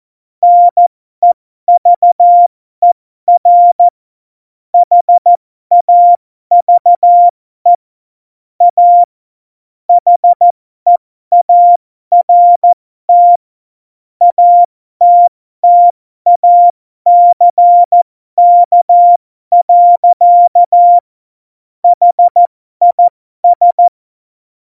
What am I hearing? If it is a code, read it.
Morse code, heard as a keyed tone, about 14 wpm, spaced as for 12 wpm, 700 Hz. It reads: NEVER HAVE A HEART ATTACK. HIS
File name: Morsekurs / Witze